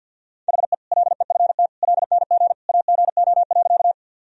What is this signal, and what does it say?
Morse code, heard as a keyed tone, about 52 wpm, 700 Hz. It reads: HE LEFT LNG AGO.
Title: Sentences from Top 500 Words and Abbreviations 50wpm